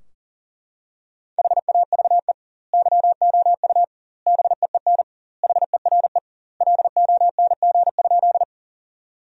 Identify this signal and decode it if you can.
Morse code, heard as a keyed tone, about 40 wpm, 700 Hz. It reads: HAVE YOU BEEN HERE LONG?
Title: 2x Word-Spacing - Sentences from Top 200 Words - 40wpm